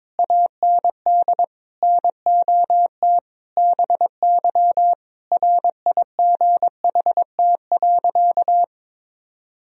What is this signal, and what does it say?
Morse code, heard as a keyed tone, about 22 wpm, 700 Hz. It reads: AND NOT BY RIGHT.
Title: Sentences from Top 1000 Words - Review Part 3 - 22wpm